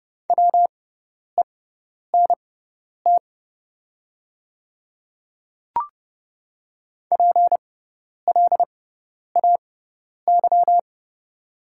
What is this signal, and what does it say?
Morse code, heard as a keyed tone, about 30 wpm, spaced as for 10 wpm, 700 Hz. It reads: WENT PLAY
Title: 6x Character Spacing - Top 200 Words - 30wpm